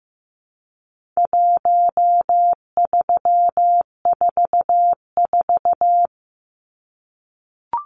Morse code, heard as a keyed tone, about 14 wpm, 700 Hz. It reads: 1344
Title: Numbers - 4-Digit 15wpm